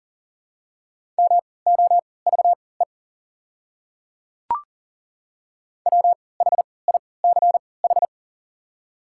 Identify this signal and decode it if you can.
Morse code, heard as a keyed tone, about 40 wpm, spaced as for 23 wpm, 690 Hz. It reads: MOVE WHICH
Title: 3x Character Spacing - Top 300 Words - 40wpm